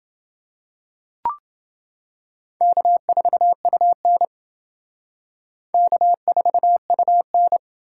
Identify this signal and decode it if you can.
Morse code, heard as a keyed tone, about 28 wpm, 700 Hz. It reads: K4UN K4UN